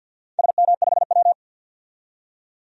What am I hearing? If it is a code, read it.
Morse code, heard as a keyed tone, about 48 wpm, 700 Hz. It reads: SM5W
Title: DXCC-100 Call Signs - 4 Characters - 50wpm